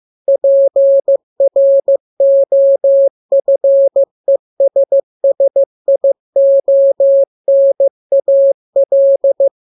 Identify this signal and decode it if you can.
Morse code, heard as a keyed tone, about 15 wpm, 550 Hz. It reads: PROFESSIONAL
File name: jokes / joke 2 15 0000